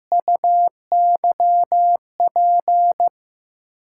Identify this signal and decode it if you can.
Morse code, heard as a keyed tone, about 15 wpm, 700 Hz. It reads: UYP
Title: USA Call Signs - 5 Characters - 8x Character Spacing - 15wpm